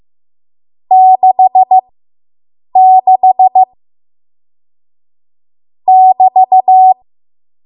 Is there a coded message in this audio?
Yes: Morse code from 0.9 to 6.9 s, heard as a keyed tone, about 15 wpm, spaced as for 7 wpm, 750 Hz. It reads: 66 <BT>